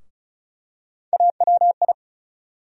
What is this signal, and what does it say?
Morse code, heard as a keyed tone, about 34 wpm, 700 Hz. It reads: AWI
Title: USA Call Sign Postfixes - 3 Letters 35wpm